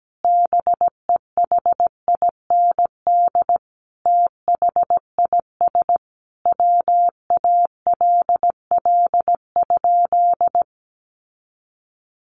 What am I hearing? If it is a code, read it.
Morse code, heard as a keyed tone, about 17 wpm, 700 Hz. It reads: BEHIND THIS WALL?
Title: Sentences from Top 1100 Words 17wpm